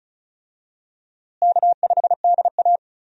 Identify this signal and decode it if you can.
Morse code, heard as a keyed tone, about 36 wpm, 700 Hz. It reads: K5DA
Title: USA Call Signs - 4 Characters - Repeated 2x - 35wpm